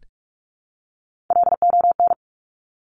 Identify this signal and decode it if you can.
Morse code, heard as a keyed tone, about 46 wpm, 700 Hz. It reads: LON
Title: Top Three Letter Combinations in Words 45wpm